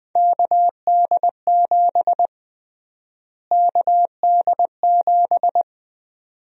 Morse code, heard as a keyed tone, about 20 wpm, 700 Hz. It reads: KD7 KD7
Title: USA Call Sign Prefixes - Repeated 2x - 20wpm